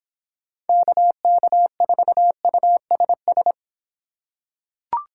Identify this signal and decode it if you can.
Morse code, heard as a keyed tone, about 26 wpm, 700 Hz. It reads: KK4USS